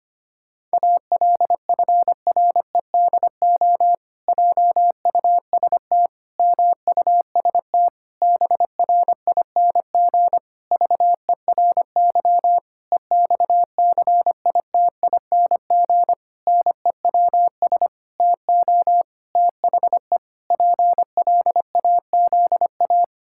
Morse code, heard as a keyed tone, about 25 wpm, 700 Hz. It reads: ALFREDO JUST MUST BRING VERY EXCITING NEWS TO THE PLAZA
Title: Pangram Challenge v2 - 25wpm